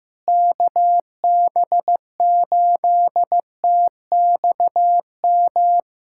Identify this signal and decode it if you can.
Morse code, heard as a keyed tone, about 16 wpm, 700 Hz. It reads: KB8TXM
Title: USA Callsign DE Callsign 15wpm